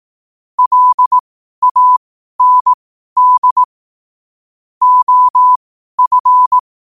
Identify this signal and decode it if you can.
Morse code, heard as a keyed tone, about 18 wpm, spaced as for 12 wpm, 1000 Hz. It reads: LAND OF
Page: Quotes for Fri, 15 Aug 2025 in Morse Code at 12 words per minute.